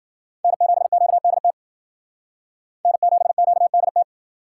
Morse code, heard as a keyed tone, about 53 wpm, 700 Hz. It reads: N7QDT N7QDT